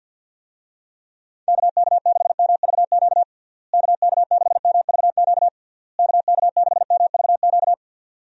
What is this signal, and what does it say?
Morse code, heard as a keyed tone, about 50 wpm, 700 Hz. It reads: KK6MVQ KK6MVQ KK6MVQ
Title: USA Call Signs - 6 Characters - Repeated 3x - 50wpm